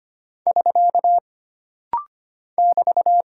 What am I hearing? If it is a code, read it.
Morse code, heard as a keyed tone, about 25 wpm, 700 Hz. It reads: <SK> <BT>
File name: Single Punctuation-Prosign - Rapid-Fire - 25wpm